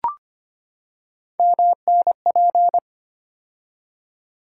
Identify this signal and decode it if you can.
Morse code, heard as a keyed tone, about 25 wpm, 700 Hz. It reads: MNP
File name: Three Letter-Number Combinations - 25wpm